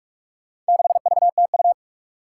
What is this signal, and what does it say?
Morse code, heard as a keyed tone, about 42 wpm, 700 Hz. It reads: 6VTU
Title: Four Challenging Letter-Number Combinations 45wpm